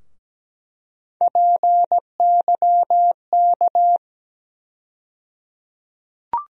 Morse code, heard as a keyed tone, about 17 wpm, 700 Hz. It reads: PYK
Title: USA Call Sign Postfixes - 3 Letters 17wpm